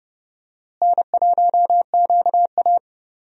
Morse code, heard as a keyed tone, about 30 wpm, 700 Hz. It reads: N1QA
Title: USA Call Signs - 4 Characters - Repeated 2x - 30wpm